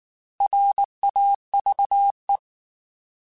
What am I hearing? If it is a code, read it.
Morse code, heard as a keyed tone, about 19 wpm, 800 Hz. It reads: RAVE
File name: Alphabetical / Alphabetical mp3s / Rs @18